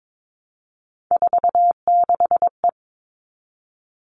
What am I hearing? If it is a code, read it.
Morse code, heard as a keyed tone, about 22 wpm, 700 Hz. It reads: 46E
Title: All Cut Numbers - 3-Digit 22wpm